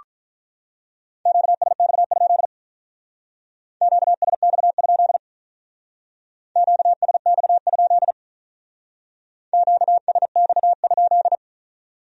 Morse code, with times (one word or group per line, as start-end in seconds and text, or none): 1.3-2.5 QSX?
3.8-5.2 QSX?
6.6-8.1 QSX?
9.5-11.3 QSX?